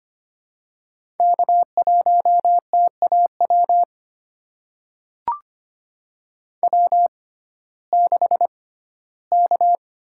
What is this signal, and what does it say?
Morse code, heard as a keyed tone, about 25 wpm, spaced as for 15 wpm, 700 Hz. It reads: K1TAW W6K
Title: USA Call Signs - 5 Characters - 6x Character Spacing - 25wpm